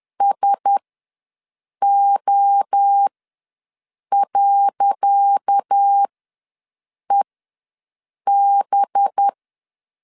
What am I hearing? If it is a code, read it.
Morse code, heard as a keyed tone, about 10 wpm, spaced as for 6 wpm, 800 Hz. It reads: SO.EB
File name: LEZIONI / 10 SIGNS / 10C DOT ws 10wpm 4213